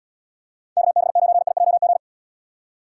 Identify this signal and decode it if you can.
Morse code, heard as a keyed tone, about 77 wpm, 700 Hz. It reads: KB9IJG